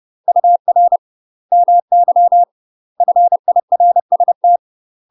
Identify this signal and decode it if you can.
Morse code, heard as a keyed tone, about 30 wpm, 700 Hz. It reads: UR MY FIRST
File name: UK MASTER QSOs TRAINING / 30WPM / UK MASTER QSOs 0033